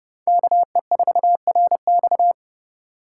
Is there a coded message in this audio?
Yes: Morse code from 0.3 to 2.3 s, heard as a keyed tone, about 29 wpm, 700 Hz. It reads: KE4RX